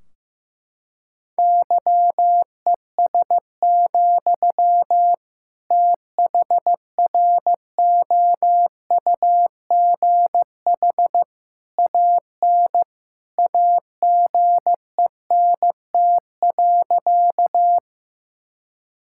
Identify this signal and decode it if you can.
Morse code, heard as a keyed tone, about 16 wpm, 700 Hz. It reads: YES, THROUGH AN AGENT.
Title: Sentences from Top 1600 Words 15wpm